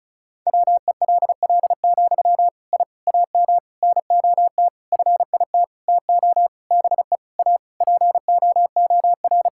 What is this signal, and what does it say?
Morse code, heard as a keyed tone, about 36 wpm, 700 Hz. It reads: WELL, I AM NOT FIT TO BE A POOR
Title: Sentences from Top 1000 Words - Review Part 5 - 35wpm